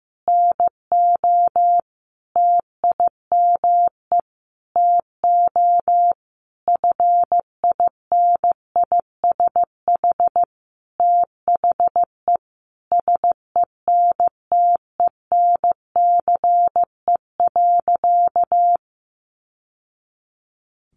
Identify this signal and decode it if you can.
Morse code, heard as a keyed tone, about 15 wpm, 700 Hz. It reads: NO TIME TO FINISH THE SENTENCE.
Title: Sentences from Top 1200 Words 15wpm